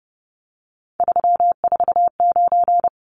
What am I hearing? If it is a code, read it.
Morse code, heard as a keyed tone, about 29 wpm, 700 Hz. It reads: 349
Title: Numbers - 3-Digit 30wpm